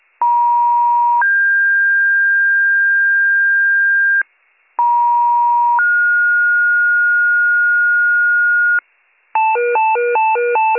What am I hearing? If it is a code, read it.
Two-tone page, heard as a one-second tone, then a three-second tone, about 950 then 1600 Hz, about 950 then 1430 Hz, with noise. It is heard from 0.2 to 8.8 s.